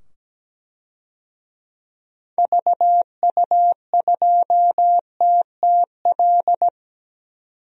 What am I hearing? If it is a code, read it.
Morse code, heard as a keyed tone, about 17 wpm, 700 Hz. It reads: VU2TTL
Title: DXCC-100 Call Signs - 6 Characters - 17wpm